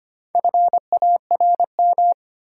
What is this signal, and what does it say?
Morse code, heard as a keyed tone, about 25 wpm, 700 Hz. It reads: FARM
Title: Sentences from Top 1000 Words - Review Part 2 - New Word Drill - 25wpm